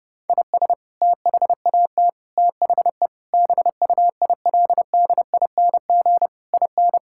Morse code, heard as a keyed tone, about 30 wpm, 700 Hz. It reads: IS THAT THE BUILDING IN
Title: Sentences from Top 1000 Words - Review Part 5 - 30wpm